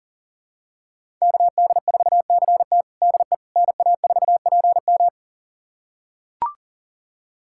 Morse code, heard as a keyed tone, about 40 wpm, 700 Hz. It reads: KD4CT DE NA4PM
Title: USA Callsign DE Callsign - 5 Characters - 40wpm